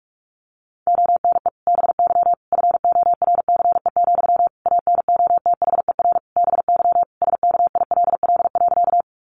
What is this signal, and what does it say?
Morse code, heard as a keyed tone, about 44 wpm, 700 Hz. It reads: ONE BY FORCE, ANOTHER BY SKILL.
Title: Sentences from Top 1200 Words 45wpm